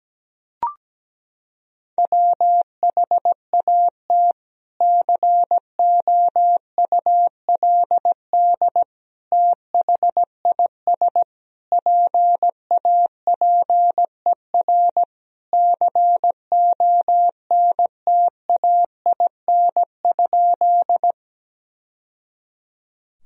Morse code, heard as a keyed tone, about 17 wpm, 700 Hz. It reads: WHAT COULD THIS PAPER CONTAIN?
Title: Sentences from Top 1000 Words - Review Part 2 - 17wpm